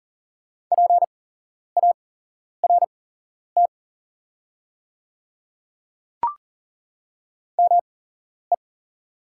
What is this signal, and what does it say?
Morse code, heard as a keyed tone, about 39 wpm, spaced as for 11 wpm, 690 Hz. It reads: PART ME
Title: 8x Character Spacing - Top 500 Words - 40wpm